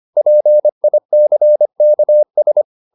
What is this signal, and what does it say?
Morse code, heard as a keyed tone, about 25 wpm, 600 Hz. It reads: PICKS